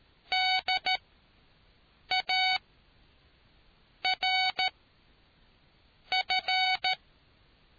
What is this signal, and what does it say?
Morse code, heard as a keyed tone, about 13 wpm, spaced as for 5 wpm, 750 Hz. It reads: DARF